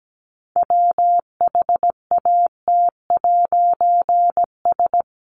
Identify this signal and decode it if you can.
Morse code, heard as a keyed tone, about 17 wpm, 700 Hz. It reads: WHAT'S
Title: A Sound Of Thunder 17wpm